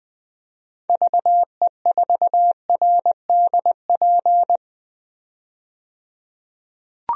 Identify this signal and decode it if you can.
Morse code, heard as a keyed tone, about 21 wpm, 700 Hz. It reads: VE4RDP
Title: DXCC-100 Call Signs - 6 Characters - 20wpm